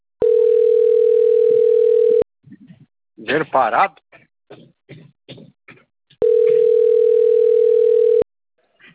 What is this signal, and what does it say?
Ringback tone (North America), 440 + 480 Hz, with speech.
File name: out- - - - - .47695